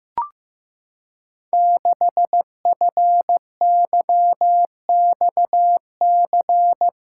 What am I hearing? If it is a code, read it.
Morse code, heard as a keyed tone, about 15 wpm, 700 Hz. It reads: 6FYXC
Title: Code Groups - 5 Letter:Numbers - 15wpm